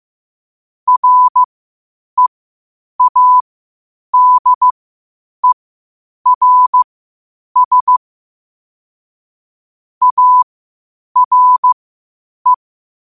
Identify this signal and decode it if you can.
Morse code, heard as a keyed tone, about 15 wpm, spaced as for 8 wpm, 1000 Hz. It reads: READERS ARE
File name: qotdfiles / 8wpm